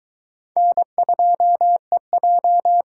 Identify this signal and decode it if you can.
Morse code, heard as a keyed tone, about 22 wpm, 700 Hz. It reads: N2EJ